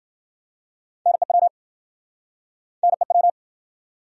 Morse code, heard as a keyed tone, about 60 wpm, 700 Hz. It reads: NEW NEW